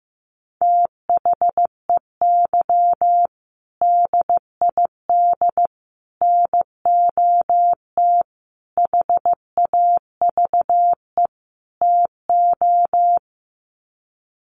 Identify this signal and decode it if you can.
Morse code, heard as a keyed tone, about 15 wpm, 700 Hz. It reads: THEY DID NOT HAVE TO
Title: Sets of 5 Words Encore2 15wpm